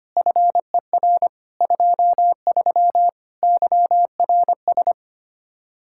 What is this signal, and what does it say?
Morse code, heard as a keyed tone, about 28 wpm, 700 Hz. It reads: FER 23 YRS
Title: Speed-Racing Common QSO Phrases - 25wpm